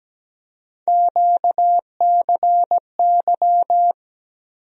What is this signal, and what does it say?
Morse code, heard as a keyed tone, about 17 wpm, 700 Hz. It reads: QCY
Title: USA Call Sign Postfixes - 3 Letters 17wpm